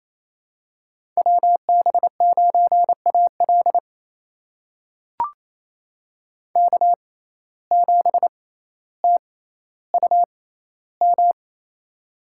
Morse code, heard as a keyed tone, about 28 wpm, spaced as for 14 wpm, 700 Hz. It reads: WB9AL K7TUM
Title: USA Call Signs - 5 Characters - 6x Character Spacing - 28wpm